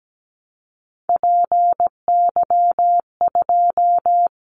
Morse code, heard as a keyed tone, about 17 wpm, 700 Hz. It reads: PY2
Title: Callsign Prefixes for Top 50 DXCC - 17wpm